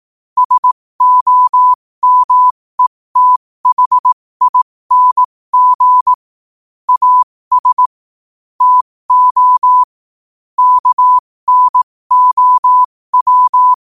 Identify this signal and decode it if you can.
Morse code, heard as a keyed tone, about 18 wpm, spaced as for 15 wpm, 1000 Hz. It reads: SOMETHING AS TO KNOW
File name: qotdfiles / 15wpm